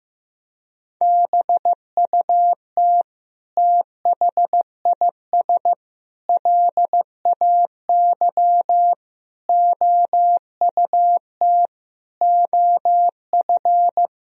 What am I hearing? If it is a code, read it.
Morse code, heard as a keyed tone, about 16 wpm, 700 Hz. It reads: BUT THIS LAY OUT OF
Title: Sentences from Top 1000 Words - Review Part 2 - 15wpm